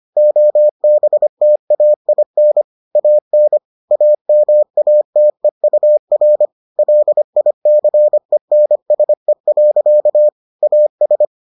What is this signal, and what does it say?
Morse code, heard as a keyed tone, about 25 wpm, 600 Hz. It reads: OBTAIN AN AMATEUR LICENSE. AS